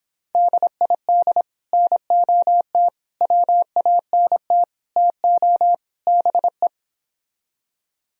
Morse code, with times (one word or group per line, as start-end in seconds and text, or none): 0.3-1.4 DID
1.7-2.9 NOT
3.2-4.6 WANT
5.0-5.7 TO
6.1-6.7 BE